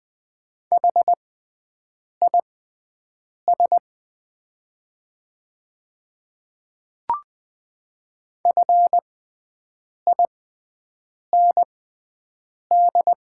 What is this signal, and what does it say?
Morse code, heard as a keyed tone, about 20 wpm, spaced as for 7 wpm, 690 Hz. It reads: HIS FIND
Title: 6x Character Spacing - Top 100 Words - 20wpm